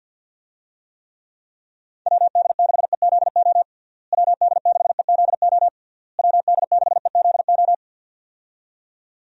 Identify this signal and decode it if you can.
Morse code, heard as a keyed tone, about 50 wpm, 700 Hz. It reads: WD6EZO WD6EZO WD6EZO
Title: USA Call Signs - 6 Characters - Repeated 3x - 50wpm